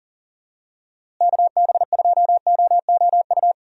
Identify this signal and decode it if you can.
Morse code, heard as a keyed tone, about 38 wpm, 700 Hz. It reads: KB2OOU
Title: USA Call Signs - 6 Characters - Repeated 2x - 40wpm